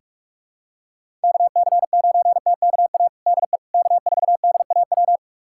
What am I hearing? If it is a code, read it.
Morse code, heard as a keyed tone, about 41 wpm, 700 Hz. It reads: KC9TKA DE K4DAW
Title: USA Callsign DE Callsign 45wpm